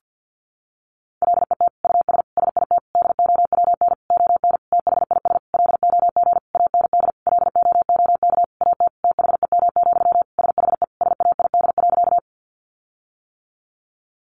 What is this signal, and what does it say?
Morse code, heard as a keyed tone, about 50 wpm, 700 Hz. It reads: LET US SIT DOWN ON THIS LOG AND LOOK AT THEM, SHE SAID.
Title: Sentences from Top 1400 Words 50wpm